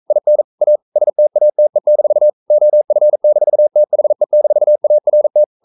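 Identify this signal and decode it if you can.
Morse code, heard as a keyed tone, about 42 wpm, 600 Hz. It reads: IN A STATE-OF-THE-ART
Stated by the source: Morse code sending of an English novel at 42 wpm.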